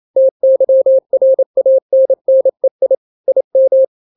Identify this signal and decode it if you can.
Morse code, heard as a keyed tone, about 28 wpm, 530 Hz. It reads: TYRANNEI IM
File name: daily / 27wpm